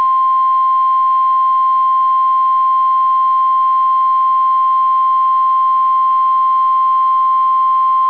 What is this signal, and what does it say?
Emergency Alert System (SAME) tones (the 1050 Hz attention tone). It is heard from 0.5 to 7.1 s.